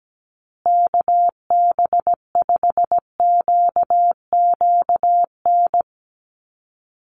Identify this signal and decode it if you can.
Morse code, heard as a keyed tone, about 17 wpm, 700 Hz. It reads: KB5QQN
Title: USA Call Signs Repeated 3x 17wpm